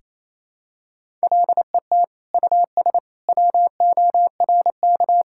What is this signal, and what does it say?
Morse code, heard as a keyed tone, about 27 wpm, 700 Hz. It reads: LET US WORK
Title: Sentences from Top 1000 Words - Review Part 5 - 28wpm